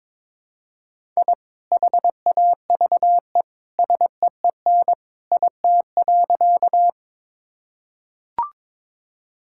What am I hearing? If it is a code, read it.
Morse code, heard as a keyed tone, about 22 wpm, 700 Hz. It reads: I HAVE SEEN IT.
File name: practice / 22wpm / Sentences with 2-Word Phrases - 22wpm